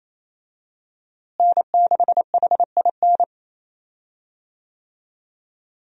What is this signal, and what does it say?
Morse code, heard as a keyed tone, about 28 wpm, 700 Hz. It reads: N6HIN
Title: USA Call Signs - 5 Characters - Repeated 2x - 28wpm